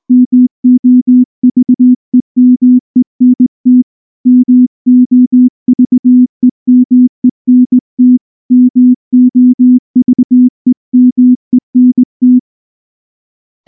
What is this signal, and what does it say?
Morse code, heard as a keyed tone, about 19 wpm, 250 Hz. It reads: MOVEMENT MOVEMENT MOVEMENT
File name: morse / morse-training-20WPM-9